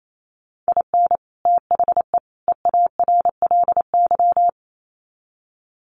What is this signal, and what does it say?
Morse code, heard as a keyed tone, about 28 wpm, 700 Hz. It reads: IN THE EARLY
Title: Sets of 3 Words 28wpm